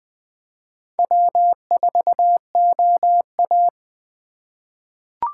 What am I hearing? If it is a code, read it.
Morse code, heard as a keyed tone, about 20 wpm, 700 Hz. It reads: W4OA
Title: USA Call Signs - 2x Character Spacing - 20wpm